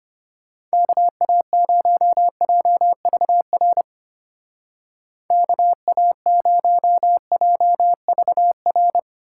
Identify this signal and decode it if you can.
Morse code, heard as a keyed tone, about 30 wpm, 700 Hz. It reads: KA0JVR KA0JVR
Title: Speed Racing - USA 2x3 Call Signs 25wpm